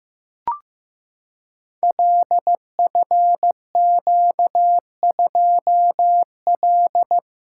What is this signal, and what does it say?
Morse code, heard as a keyed tone, about 15 wpm, 700 Hz. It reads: LFQ2L
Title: Code Groups - 5 Letter:Numbers - 15wpm